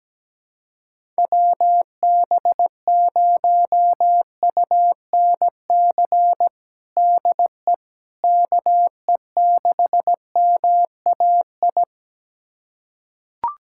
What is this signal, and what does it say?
Morse code, heard as a keyed tone, about 17 wpm, 700 Hz. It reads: WB0UNC DE KE6MAI
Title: USA Callsign DE Callsign 17wpm